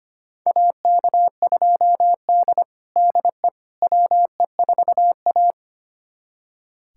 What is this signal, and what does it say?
Morse code, heard as a keyed tone, about 25 wpm, 700 Hz. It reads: AK2D DE WE4A
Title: USA Callsign DE Callsign - 4 Characters - 25wpm